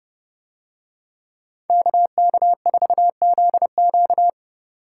Morse code, heard as a keyed tone, about 28 wpm, 700 Hz. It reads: KK4ZQ